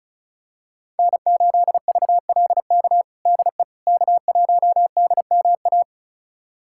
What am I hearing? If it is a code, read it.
Morse code, heard as a keyed tone, about 34 wpm, 700 Hz. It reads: N8VLK DE K1DMA
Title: USA Callsign DE Callsign - 5 Characters - 35wpm